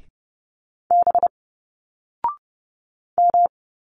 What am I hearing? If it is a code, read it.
Morse code, heard as a keyed tone, about 30 wpm, 700 Hz. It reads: B M
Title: Single Letters 30wpm